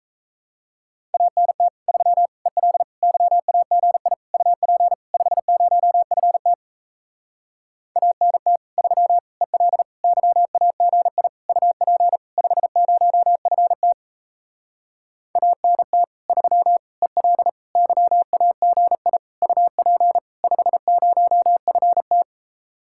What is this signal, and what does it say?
Morse code, heard as a keyed tone, about 37 wpm, 700 Hz. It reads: ANT 3 EL YAGI UP 50FT ANT 3 EL YAGI UP 50FT ANT 3 EL YAGI UP 50FT